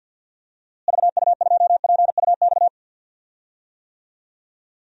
Morse code, heard as a keyed tone, about 52 wpm, 700 Hz. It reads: VU2PUX
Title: DXCC-100 Call Signs - 6 Characters - Repeated 3x - 50wpm